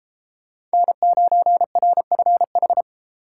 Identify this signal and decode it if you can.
Morse code, heard as a keyed tone, about 30 wpm, 700 Hz. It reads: N9RFH